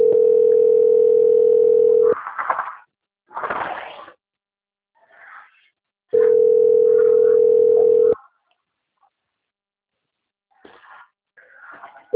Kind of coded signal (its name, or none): telephone call progress tone